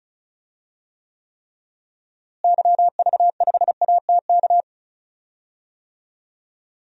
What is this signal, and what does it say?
Morse code, heard as a keyed tone, about 36 wpm, 700 Hz. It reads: YV5ATK
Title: DXCC-100 Call Signs - 6 Characters - Repeated 3x - 35wpm